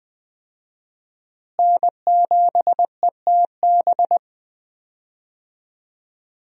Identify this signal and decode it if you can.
Morse code, heard as a keyed tone, about 20 wpm, 700 Hz. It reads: N7ETB
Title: USA Call Signs - 5 Characters - Repeated 2x - 20wpm